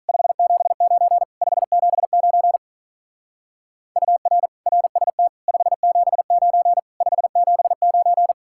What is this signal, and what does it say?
Morse code, heard as a keyed tone, about 44 wpm, 700 Hz. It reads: 579 579 UR RST 579 579